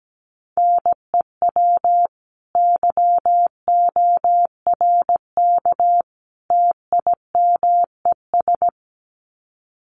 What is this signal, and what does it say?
Morse code, heard as a keyed tone, about 17 wpm, 700 Hz. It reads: NEW YORK TIMES
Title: Sets of 3 Words 17wpm